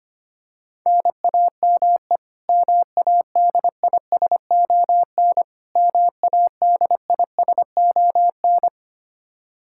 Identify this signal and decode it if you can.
Morse code, heard as a keyed tone, about 25 wpm, 700 Hz. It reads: NAME MADISON MADISON
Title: K1USN SST QSO Practice 25wpm